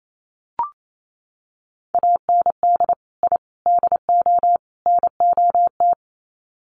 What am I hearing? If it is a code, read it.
Morse code, heard as a keyed tone, about 28 wpm, 700 Hz. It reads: AND I DO NOT
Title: Sets of 4 Words 28wpm